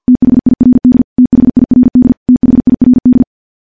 Morse code, heard as a keyed tone, about 50 wpm, 250 Hz. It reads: THIRD THIRD THIRD
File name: morse-training-50WPM